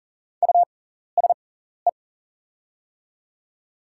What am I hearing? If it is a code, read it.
Morse code, heard as a keyed tone, about 40 wpm, spaced as for 14 wpm, 690 Hz. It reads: USE
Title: 6x Character Spacing - Top 500 Words - 40wpm